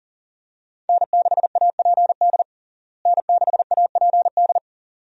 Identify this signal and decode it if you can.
Morse code, heard as a keyed tone, about 40 wpm, 700 Hz. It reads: N6APD N6APD
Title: USA Call Signs - 5 Characters - Repeated 2x - 40wpm